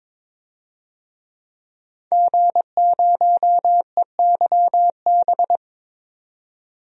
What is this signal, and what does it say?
Morse code, heard as a keyed tone, about 22 wpm, 700 Hz. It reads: G0EYB
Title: DXCC-100 Call Signs - 5 Characters - 22wpm